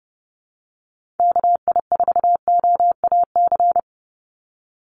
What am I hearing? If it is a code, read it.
Morse code, heard as a keyed tone, about 30 wpm, 700 Hz. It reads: KI4OAC
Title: USA Call Signs 30wpm